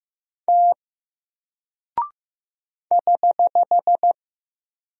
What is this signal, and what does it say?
Morse code, heard as a keyed tone, about 15 wpm, 700 Hz. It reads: T <HH>
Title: Single Character (Letter:Number:Punctuation:Prosign) Alt 15wpm